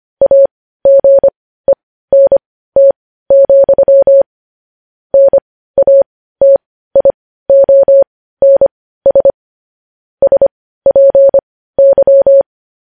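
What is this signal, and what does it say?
Morse code, heard as a keyed tone, about 25 wpm, spaced as for 15 wpm, 550 Hz. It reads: AGENT, NATIONS SPY
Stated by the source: Morse news headlines sent at 15 words per minute.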